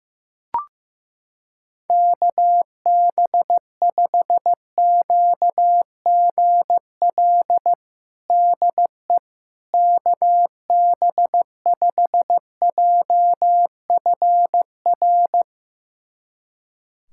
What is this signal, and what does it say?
Morse code, heard as a keyed tone, about 15 wpm, 700 Hz. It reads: KB5QGL DE KB5JFR
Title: USA Callsign DE Callsign - 6 Characters - 15wpm